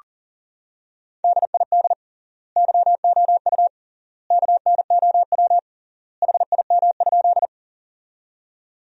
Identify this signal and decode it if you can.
Morse code, heard as a keyed tone, about 40 wpm, 700 Hz. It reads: DID YOU KNOW HIM?
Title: 3x Word-Spacing - Sentences from Top 100 Words - 40wpm